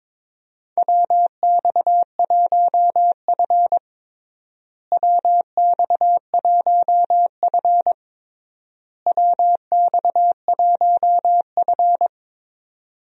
Morse code, heard as a keyed tone, about 22 wpm, 700 Hz. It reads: WX1F WX1F WX1F
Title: USA Call Signs - 4 Characters - Repeated 3x - 22wpm